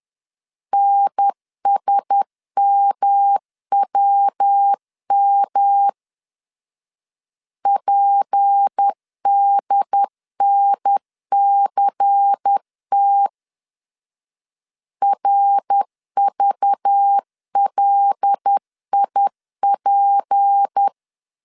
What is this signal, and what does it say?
Morse code, heard as a keyed tone, about 10 wpm, 800 Hz. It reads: NSMWM PDNCT RVLIP
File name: CW / DET / 1 6 1500C S B 10wpm 4137